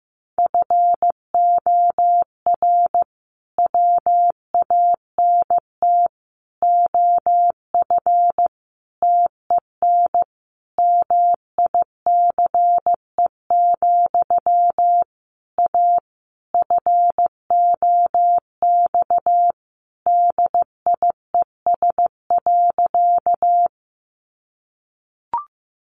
Morse code, heard as a keyed tone, about 14 wpm, 700 Hz. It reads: FOR WANT OF TEN MICE, A FOX DIES.
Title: A Sound Of Thunder 15wpm